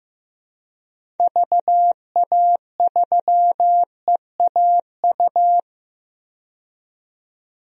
Morse code, heard as a keyed tone, about 15 wpm, 700 Hz. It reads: VA3EAU
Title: DXCC-100 Call Signs - 6 Characters - 15wpm